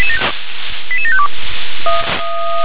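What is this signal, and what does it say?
Five-tone selective call (ZVEI), heard as quick runs of five tones, tones of about 70 ms. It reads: 88752
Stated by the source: ZVEI five-tone call 88722 received by SDR.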